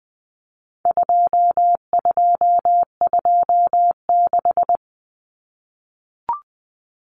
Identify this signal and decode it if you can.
Morse code, heard as a keyed tone, about 20 wpm, 700 Hz. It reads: 2226
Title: Common Cut Numbers - 4-Digit 20wpm